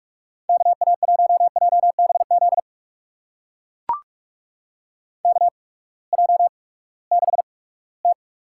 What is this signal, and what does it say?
Morse code, heard as a keyed tone, about 44 wpm, spaced as for 22 wpm, 700 Hz. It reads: KA1JBZ KJ6T
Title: USA Call Signs - 6 Characters - 8x Character Spacing - 45wpm